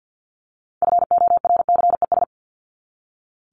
Morse code, heard as a keyed tone, about 50 wpm, 700 Hz. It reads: FORCES
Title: ES:S Suffix - Top 100 Words with Base Words 50wpm